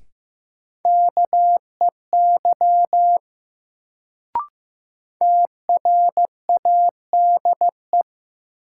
Morse code, heard as a keyed tone, about 15 wpm, 700 Hz. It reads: KEY TRADE
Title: Sentences from Top 600 Words - New Word Drill - 15wpm